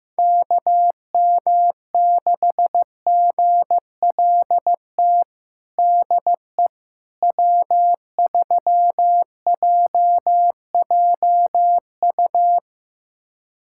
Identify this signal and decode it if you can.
Morse code, heard as a keyed tone, about 16 wpm, 700 Hz. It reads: KM6GLT DE W3JJU
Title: USA Callsign DE Callsign 15wpm